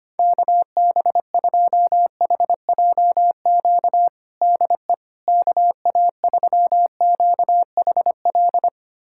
Morse code, heard as a keyed tone, about 25 wpm, 700 Hz. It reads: KB2HJQ DE KA3QHL
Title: USA Callsign DE Callsign 25wpm